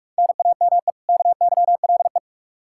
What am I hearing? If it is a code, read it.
Morse code, heard as a keyed tone, about 46 wpm, 700 Hz. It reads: NAME KYLE
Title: Practice QSO - Exchange 1 - Beginning - 45wpm